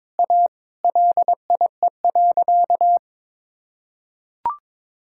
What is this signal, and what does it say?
Morse code, heard as a keyed tone, about 22 wpm, 700 Hz. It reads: A LIE.
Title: Sentences from Top 1600 Words 22wpm